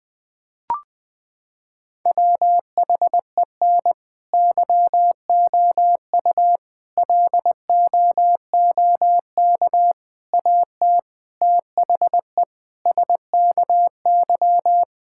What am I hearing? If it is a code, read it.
Morse code, heard as a keyed tone, about 20 wpm, 700 Hz. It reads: WHEN YOU LOOK AT THE SKY
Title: Sentences with 3-Word Phrases - 20wpm